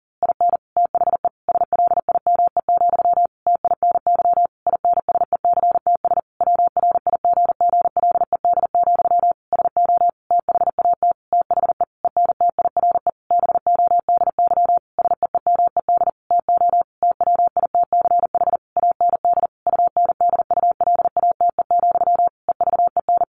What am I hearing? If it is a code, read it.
Morse code, heard as a keyed tone, about 40 wpm, 700 Hz. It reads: IN THE SLIME, TINY INSECTS WRIGGLED, SO THAT THE ENTIRE BODY SEEMED TO TWITCH AND UNDULATE, EVEN
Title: A Sound Of Thunder 40wpm